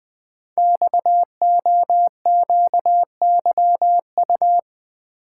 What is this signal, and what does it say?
Morse code, heard as a keyed tone, about 20 wpm, 700 Hz. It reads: XOQYU
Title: Code Groups - 5 Letters - 20wpm